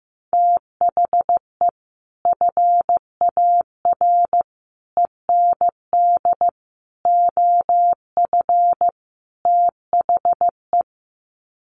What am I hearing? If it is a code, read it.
Morse code, heard as a keyed tone, about 15 wpm, 700 Hz. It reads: THE FAR END OF THE
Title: Sets of 5 Words 15wpm